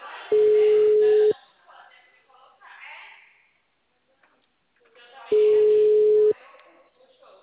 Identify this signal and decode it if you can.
Ringback tone (Europe, 425 Hz), with speech.